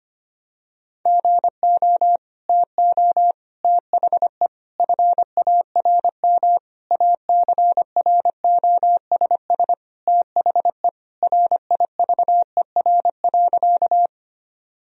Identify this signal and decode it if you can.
Morse code, heard as a keyed tone, about 25 wpm, 700 Hz. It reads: GO TO THE FARM ACROSS THE RIVER.
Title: Sentences from Top 1000 Words - Review Part 2 - 25wpm